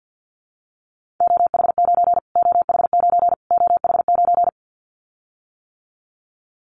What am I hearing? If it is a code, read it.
Morse code, heard as a keyed tone, about 50 wpm, 700 Hz. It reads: OH9 OH9 OH9
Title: Callsign Prefixes for Top 25 DXCC - Repeated 3x - 50wpm